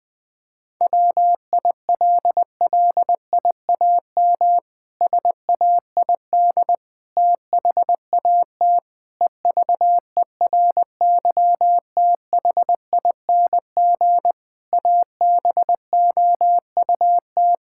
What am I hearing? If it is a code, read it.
Morse code, heard as a keyed tone, about 21 wpm, 700 Hz. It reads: WILLIAM SAID THAT EVERYTHING ABOUT
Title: Pangram Challenge v2 - 20wpm